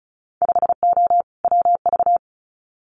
Morse code, heard as a keyed tone, about 36 wpm, 700 Hz. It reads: 5O WV
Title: Winter Field Day Exchanges Encore 35wpm